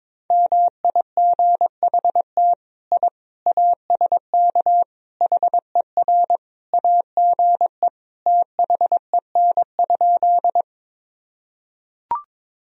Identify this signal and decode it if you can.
Morse code, heard as a keyed tone, about 21 wpm, 700 Hz. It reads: MIGHT I ASK HER AGE THEN?
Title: Sentences from Top 1000 Words - Review Part 3 - 22wpm